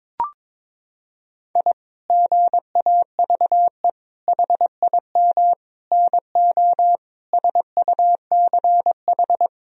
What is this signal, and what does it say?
Morse code, heard as a keyed tone, about 21 wpm, 700 Hz. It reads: I GAVE HIM NO SUCH
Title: Sentences from Top 1000 Words - Review Part 2 - 22wpm